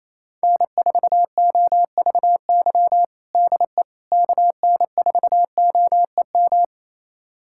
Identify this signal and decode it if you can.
Morse code, heard as a keyed tone, about 28 wpm, 700 Hz. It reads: N4OVY DE KN4OEM
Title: USA Callsign DE Callsign 28wpm